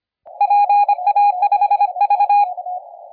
Morse code, heard as a keyed tone, about 25 wpm, 820 Hz. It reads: PA5V
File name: la4c / sac-cw-2017 / CONTEST AUDIO 1100Z